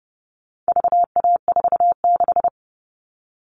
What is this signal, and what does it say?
Morse code, heard as a keyed tone, about 30 wpm, 700 Hz. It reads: VA46
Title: All Cut Numbers - 4-Digit 30wpm